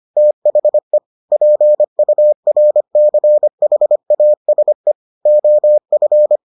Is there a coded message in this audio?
Yes: Morse code from 0.2 to 6.4 s, heard as a keyed tone, about 25 wpm, 600 Hz. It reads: THE PURCHASE OF